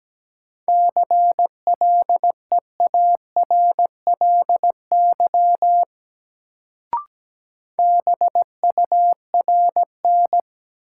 Morse code, heard as a keyed tone, about 17 wpm, 700 Hz. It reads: CLEARLY BURN
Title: Sentences from Top 1300 Words - New Word Drill Alt - 17wpm